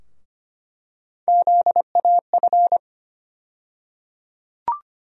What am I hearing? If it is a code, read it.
Morse code, heard as a keyed tone, about 25 wpm, 700 Hz. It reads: ZAF
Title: USA Call Sign Postfixes - 3 Letters 25wpm